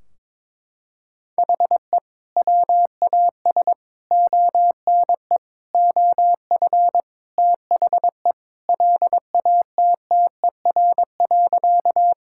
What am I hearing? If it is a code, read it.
Morse code, heard as a keyed tone, about 22 wpm, 700 Hz. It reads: HE WAS ONE OF THE LATTER.
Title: Sentences from Top 1600 Words 22wpm